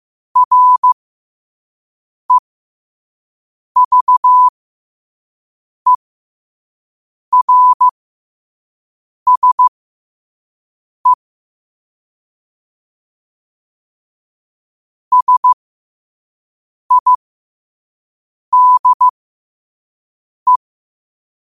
Morse code, heard as a keyed tone, about 15 wpm, spaced as for 5 wpm, 1000 Hz. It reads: REVERSE SIDE
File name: qotdfiles / 5wpm